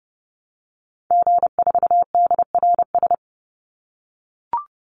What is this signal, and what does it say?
Morse code, heard as a keyed tone, about 30 wpm, 700 Hz. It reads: G4DRS
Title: Call Signs - 30wpm